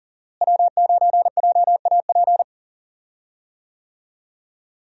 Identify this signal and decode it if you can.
Morse code, heard as a keyed tone, about 39 wpm, 700 Hz. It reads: W9JAP
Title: USA Call Signs - 5 Characters - Repeated 3x - 40wpm